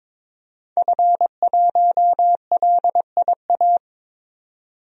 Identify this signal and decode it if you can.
Morse code, heard as a keyed tone, about 22 wpm, 700 Hz. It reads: F1LIA
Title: DXCC-100 Call Signs - 5 Characters - 22wpm